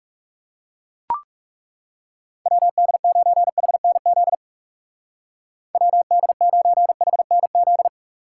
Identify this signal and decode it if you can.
Morse code, heard as a keyed tone, about 37 wpm, 700 Hz. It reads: WD9HNZ WD9HNZ
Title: Speed Racing - USA 2x3 Call Signs 30wpm